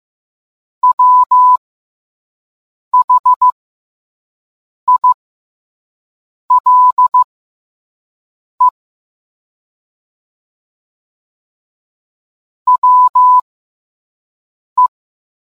Morse code, heard as a keyed tone, about 15 wpm, spaced as for 5 wpm, 1000 Hz. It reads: WHILE WE